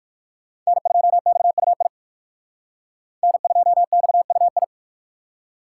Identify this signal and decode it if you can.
Morse code, heard as a keyed tone, about 57 wpm, 700 Hz. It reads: N2XUI N2XUI